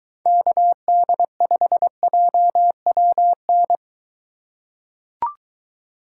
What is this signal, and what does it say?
Morse code, heard as a keyed tone, about 23 wpm, 700 Hz. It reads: KD5JWN